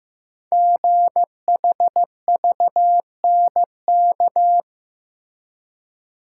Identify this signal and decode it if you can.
Morse code, heard as a keyed tone, about 15 wpm, 700 Hz. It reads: GHVNK
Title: Code Groups - 5 Letters - 15wpm